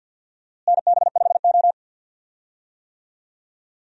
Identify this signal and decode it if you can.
Morse code, heard as a keyed tone, about 53 wpm, 700 Hz. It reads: NB5O
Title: USA Call Signs - 4 Characters - Repeated 3x - 50wpm